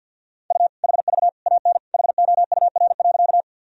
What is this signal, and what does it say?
Morse code, heard as a keyed tone, about 51 wpm, 700 Hz. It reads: U HV AN HOUR.
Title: Sentences from Top 400 Words and Abbreviations 50wpm